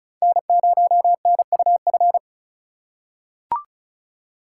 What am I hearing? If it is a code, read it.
Morse code, heard as a keyed tone, about 36 wpm, 700 Hz. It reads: N0NUF
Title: USA Call Signs - 5 Characters - 4x Character Spacing - 35wpm